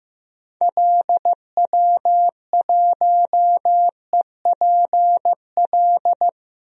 Morse code, heard as a keyed tone, about 15 wpm, 700 Hz. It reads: LW1EPL
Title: DXCC-100 Call Signs - 6 Characters - 15wpm